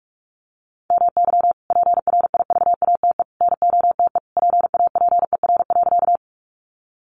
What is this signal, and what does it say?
Morse code, heard as a keyed tone, about 44 wpm, 700 Hz. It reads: MY PRIVATE NOTE PAPER.
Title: Sentences from Top 1000 Words 45wpm